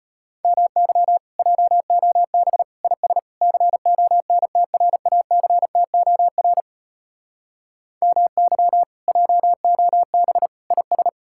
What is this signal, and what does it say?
Morse code, heard as a keyed tone, about 33 wpm, 700 Hz. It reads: MY JOB IS CONTRACTOR MY JOB IS